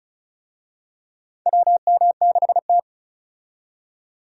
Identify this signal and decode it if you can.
Morse code, heard as a keyed tone, about 34 wpm, 700 Hz. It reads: WM6T
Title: USA Call Signs - 4 Characters - 35wpm